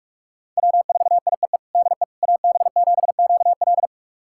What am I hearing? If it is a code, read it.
Morse code, heard as a keyed tone, about 46 wpm, 700 Hz. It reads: W4IEE DE AB7QL
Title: USA Callsign DE Callsign - 5 Characters - 45wpm